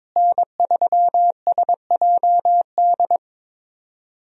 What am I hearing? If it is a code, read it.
Morse code, heard as a keyed tone, about 22 wpm, 700 Hz. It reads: N3SJD
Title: USA Call Signs - 5 Characters - Repeated 3x - 22wpm